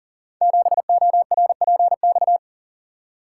Morse code, heard as a keyed tone, about 38 wpm, 700 Hz. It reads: 7ORPX
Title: Code Groups - 5 Letter:Numbers - 40wpm